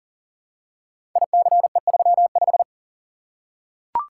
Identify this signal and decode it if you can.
Morse code, heard as a keyed tone, about 39 wpm, 700 Hz. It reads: ICE35
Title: Code Groups - 5 Letter:Numbers - 40wpm